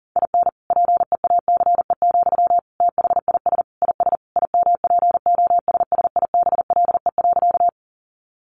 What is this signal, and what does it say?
Morse code, heard as a keyed tone, about 40 wpm, 700 Hz. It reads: IN PEACE, THIS IS IMPOSSIBLE.
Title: Sentences from Top 1500 Words 40wpm